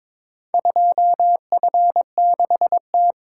Morse code, heard as a keyed tone, about 22 wpm, 700 Hz. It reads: 2F6T
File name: practice / 22wpm / Four Letter-Number Combinations - 22wpm